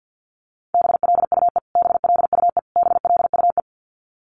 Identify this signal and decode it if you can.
Morse code, heard as a keyed tone, about 50 wpm, 700 Hz. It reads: BLUE BLUE BLUE
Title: Top 500 Words Repeated 50wpm